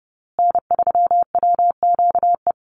Morse code, heard as a keyed tone, about 29 wpm, 700 Hz. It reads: N3WQE
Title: USA Call Signs Repeated 2x 30wpm